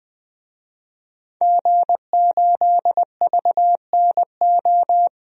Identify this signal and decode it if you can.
Morse code, heard as a keyed tone, about 20 wpm, 700 Hz. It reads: G8VNO
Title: DXCC-100 Call Signs - 5 Characters - 20wpm